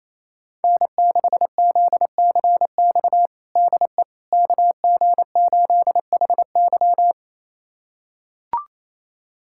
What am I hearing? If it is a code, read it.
Morse code, heard as a keyed tone, about 28 wpm, 700 Hz. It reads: N6ZCX DE KG8HY
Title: USA Callsign DE Callsign 28wpm